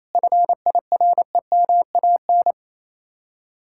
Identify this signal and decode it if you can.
Morse code, heard as a keyed tone, about 37 wpm, 700 Hz. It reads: FIREMAN